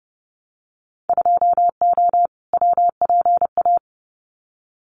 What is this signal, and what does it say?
Morse code, heard as a keyed tone, about 30 wpm, 700 Hz. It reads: 2O WPA
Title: Winter Field Day Exchanges Encore 30wpm